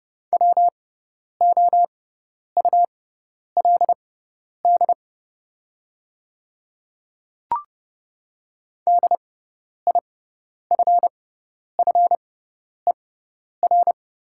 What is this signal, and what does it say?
Morse code, heard as a keyed tone, about 30 wpm, spaced as for 10 wpm, 690 Hz. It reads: WOULD DIFFER
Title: 6x Character Spacing - Top 500 Words - 30wpm